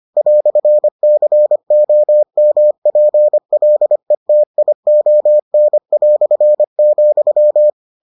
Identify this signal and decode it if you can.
Morse code, heard as a keyed tone, about 25 wpm, 600 Hz. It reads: "COMPLETION",